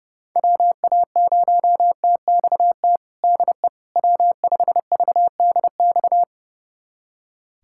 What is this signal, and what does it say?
Morse code, heard as a keyed tone, about 32 wpm, 700 Hz. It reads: WA0TXT DE W5VDX
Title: USA Callsign DE Callsign 30wpm